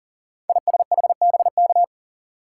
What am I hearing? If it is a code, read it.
Morse code, heard as a keyed tone, about 40 wpm, 700 Hz. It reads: ISHBK
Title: Code Groups - 5 Letters - 40wpm